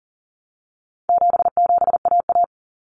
Morse code, heard as a keyed tone, about 40 wpm, 700 Hz. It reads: 77AA
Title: Common Cut Numbers - 4-Digit 40wpm